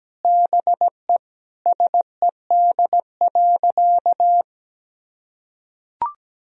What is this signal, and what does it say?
Morse code, heard as a keyed tone, about 17 wpm, 700 Hz. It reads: BE SED.
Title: Sentences from Top 500 Words and Abbreviations 17wpm